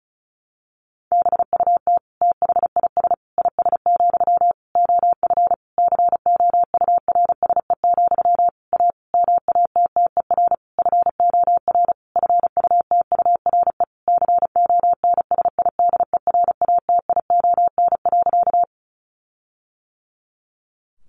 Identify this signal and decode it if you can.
Morse code, heard as a keyed tone, about 34 wpm, 700 Hz. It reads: BUT THIS IS, OF COURSE, A MATTER FOR FUTURE CONSIDERATION.
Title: Sentences from Top 1500 Words 35wpm